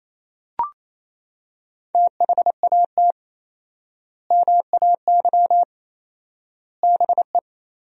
Morse code, heard as a keyed tone, about 28 wpm, 700 Hz. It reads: THAT MAY BE
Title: 4x Word-Spacing - Sentences from Top 400 Words - 28wpm